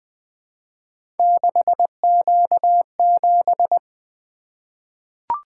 Morse code, heard as a keyed tone, about 21 wpm, 700 Hz. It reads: 6Q7
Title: Three Letter-Number Combinations - 20wpm